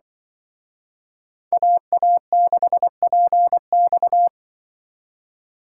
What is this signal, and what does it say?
Morse code, heard as a keyed tone, about 26 wpm, 700 Hz. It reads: AA6PX